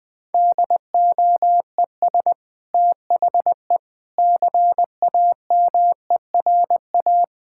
Morse code, heard as a keyed tone, about 21 wpm, 700 Hz. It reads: DOES THE CAMERA
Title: Sentences from Top 1000 Words - Review Part 5 - 20wpm